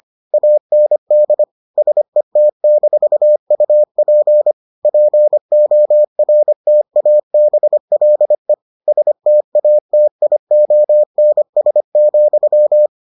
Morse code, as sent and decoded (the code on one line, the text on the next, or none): .- -. -.. / ... - -....- ..- .--. / .--. --- .-. - .- -... .-.. / ... - .- - .. --- -. ... --..--
AND SET-UP PORTABLE STATIONS,